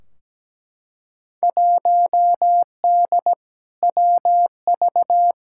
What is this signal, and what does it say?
Morse code, heard as a keyed tone, about 17 wpm, 700 Hz. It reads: 1D WV
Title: Field Day Exchanges - 17wpm